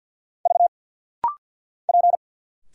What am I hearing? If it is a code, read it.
Morse code, heard as a keyed tone, about 50 wpm, 700 Hz. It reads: V P
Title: Single Letters - Rapid-Fire 50wpm